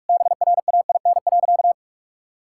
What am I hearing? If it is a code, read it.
Morse code, heard as a keyed tone, about 46 wpm, 690 Hz. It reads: BRAIN.
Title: Sentences from Top 1000 Words - Review Part 6 - 45wpm